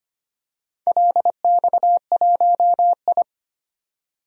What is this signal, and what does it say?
Morse code, heard as a keyed tone, about 25 wpm, 700 Hz. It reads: LX1I
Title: DXCC-100 Call Signs - 4 Characters - 25wpm